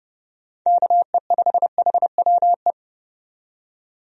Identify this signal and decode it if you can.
Morse code, heard as a keyed tone, about 30 wpm, 700 Hz. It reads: KE5HWE
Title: USA Call Signs - 6 Characters - 30wpm